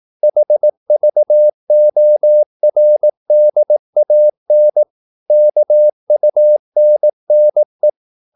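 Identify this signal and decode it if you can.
Morse code, heard as a keyed tone, about 18 wpm, 600 Hz. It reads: HVORDAN KUNNE
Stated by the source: Slow (QRS) Morse practice at 18 wpm, CW med Gnister.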